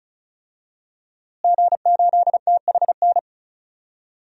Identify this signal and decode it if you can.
Morse code, heard as a keyed tone, about 36 wpm, 700 Hz. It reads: G8THN
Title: DXCC-100 Call Signs - 5 Characters - Repeated 2x - 35wpm